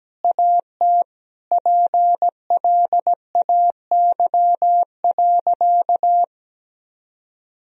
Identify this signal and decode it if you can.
Morse code, heard as a keyed tone, about 17 wpm, 700 Hz. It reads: AT PLAY.
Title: Sentences from Top 1000 Words - Review Part 4 - 17wpm